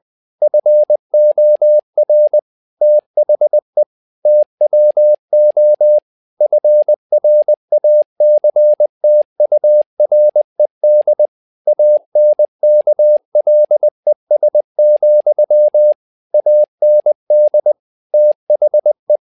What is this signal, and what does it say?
Morse code, heard as a keyed tone, about 20 wpm, 600 Hz. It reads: FOR THE TWO FRACTURED ANKLES, AND THE
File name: tower trouble 20wpm